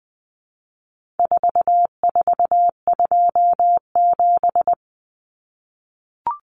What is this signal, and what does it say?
Morse code, heard as a keyed tone, about 20 wpm, 700 Hz. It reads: 4427
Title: Numbers - 4-Digit 20wpm